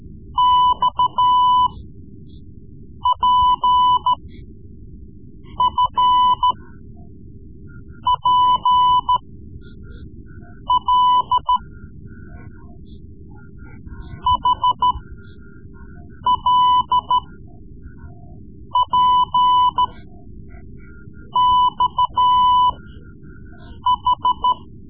Morse code, heard as a keyed tone, about 13 wpm, spaced as for 5 wpm, 1000 Hz. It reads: XPFPL HLPXH
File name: Morse Lessons / Morse Lessons / Tape-3B